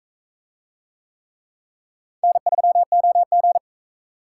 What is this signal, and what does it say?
Morse code, heard as a keyed tone, about 43 wpm, 700 Hz. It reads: N3OG